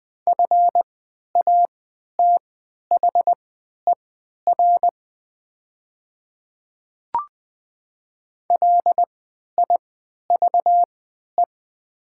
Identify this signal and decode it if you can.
Morse code, heard as a keyed tone, about 20 wpm, spaced as for 11 wpm, 700 Hz. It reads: FATHER LIVE
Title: 3x Character Spacing - Top 200 Words - 20wpm